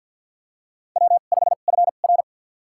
Morse code, heard as a keyed tone, about 51 wpm, spaced as for 36 wpm, 700 Hz. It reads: W5FR
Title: USA Call Signs - 4 Characters - 2x Character Spacing - 50wpm